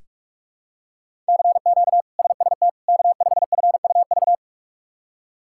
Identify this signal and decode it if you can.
Morse code, heard as a keyed tone, about 42 wpm, 700 Hz. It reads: CQ SST K5FUV
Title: K1USN SST QSO Practice - 45wpm